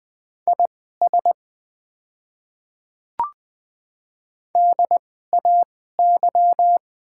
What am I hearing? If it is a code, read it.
Morse code, heard as a keyed tone, about 20 wpm, spaced as for 14 wpm, 700 Hz. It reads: IS DAY
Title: 2x Character Spacing - Top 200 Words - 20wpm